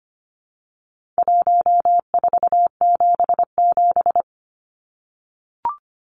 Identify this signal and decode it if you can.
Morse code, heard as a keyed tone, about 25 wpm, 700 Hz. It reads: 1477
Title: Numbers - Contest Style - 4-Digit 25wpm